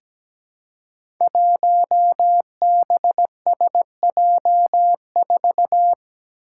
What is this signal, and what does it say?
Morse code, heard as a keyed tone, about 17 wpm, 700 Hz. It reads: 1BSJ4
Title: Five Challenging Letter-Number Combinations 17wpm